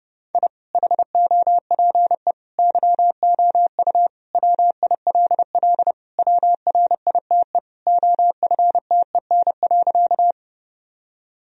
Morse code, heard as a keyed tone, about 30 wpm, 700 Hz. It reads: I HOPE YOU WILL WRITE OFTEN.
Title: Sentences from Top 1000 Words - Review Part 1 - 30wpm